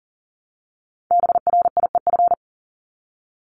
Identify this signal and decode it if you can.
Morse code, heard as a keyed tone, about 40 wpm, 700 Hz. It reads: BRIEF
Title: LY Suffix - Top 100 Words with Base Words 40wpm